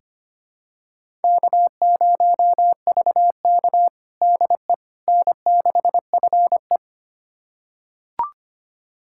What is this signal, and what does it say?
Morse code, heard as a keyed tone, about 25 wpm, 700 Hz. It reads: K0VK DE N6FE
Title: USA Callsign DE Callsign - 4 Characters - 25wpm